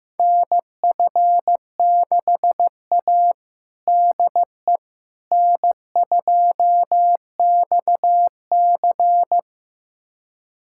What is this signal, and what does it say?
Morse code, heard as a keyed tone, about 15 wpm, 700 Hz. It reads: NF6A DE N2XC
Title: USA Callsign DE Callsign - 4 Characters - 15wpm